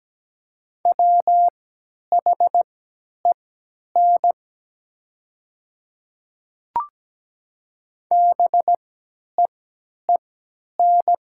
Morse code, heard as a keyed tone, about 17 wpm, spaced as for 10 wpm, 700 Hz. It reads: WHEN BEEN
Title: 3x Character Spacing - Top 100 Words - 17wpm